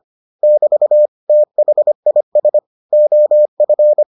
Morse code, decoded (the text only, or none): <BT> THIS OF